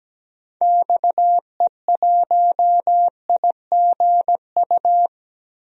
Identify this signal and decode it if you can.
Morse code, heard as a keyed tone, about 17 wpm, 700 Hz. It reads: XE1IGU
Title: DXCC-100 Call Signs - 6 Characters - Repeated 3x - 17wpm